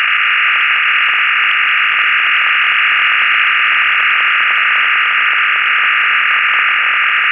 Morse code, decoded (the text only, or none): none